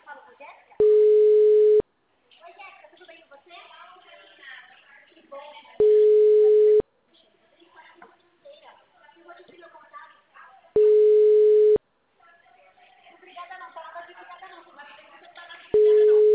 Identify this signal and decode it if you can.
Ringback tone (Europe, 425 Hz), with speech.
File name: out- -2007- - - .8852